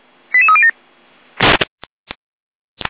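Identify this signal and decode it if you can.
Five-tone selective call (ZVEI), heard as quick runs of five tones, tones of about 70 ms. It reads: 88307
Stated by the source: ZVEI five-tone call 88171 received by SDR.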